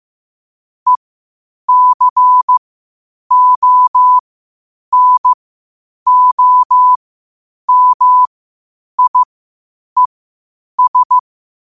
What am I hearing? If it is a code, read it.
Morse code, heard as a keyed tone, about 15 wpm, spaced as for 8 wpm, 1000 Hz. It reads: ECONOMIES
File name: qotdfiles / 8wpm